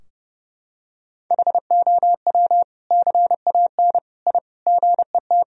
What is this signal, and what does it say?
Morse code, heard as a keyed tone, about 30 wpm, 700 Hz. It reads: HOW CAN I GET
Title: Sentences from Top 1000 Words - Review Part 3 - 30wpm